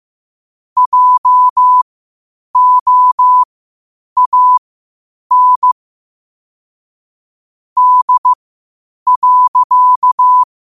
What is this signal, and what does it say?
Morse code, heard as a keyed tone, about 15 wpm, spaced as for 8 wpm, 1000 Hz. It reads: JOAN D.
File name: qotdfiles / 8wpm